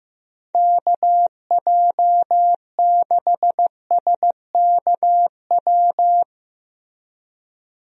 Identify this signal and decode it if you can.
Morse code, heard as a keyed tone, about 16 wpm, 700 Hz. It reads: KJ6SKW
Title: USA Call Signs - 6 Characters - Repeated 3x - 15wpm